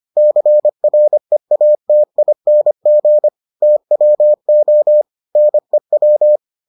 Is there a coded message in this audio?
Yes: Morse code from 0.2 to 6.4 s, heard as a keyed tone, about 25 wpm, 600 Hz. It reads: CREATING TWO NEW